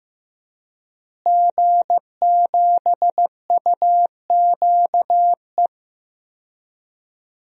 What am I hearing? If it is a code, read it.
Morse code, heard as a keyed tone, about 15 wpm, 700 Hz. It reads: G7UQE
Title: DXCC-100 Call Signs - 5 Characters - Repeated 2x - 15wpm